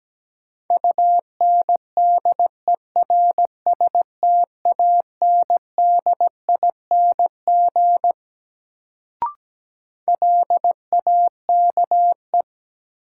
Morse code, heard as a keyed tone, about 17 wpm, 700 Hz. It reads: UNDERSTANDING LAKE
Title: Sentences from Top 1000 Words - Review Part 3 - New Word Drill - 17wpm